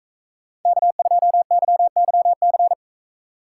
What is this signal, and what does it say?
Morse code, heard as a keyed tone, about 44 wpm, 700 Hz. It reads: K2YYC